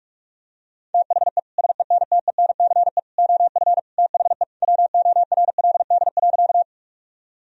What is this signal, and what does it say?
Morse code, heard as a keyed tone, about 46 wpm, 700 Hz. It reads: THE SENTENCE OF THE WORLD.
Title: Sentences from Top 1000 Words - Review Part 1 - 45wpm